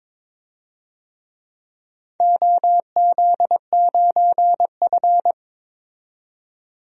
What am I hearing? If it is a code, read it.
Morse code, heard as a keyed tone, about 22 wpm, 700 Hz. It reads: OZ9F
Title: DXCC-100 Call Signs - 4 Characters - 22wpm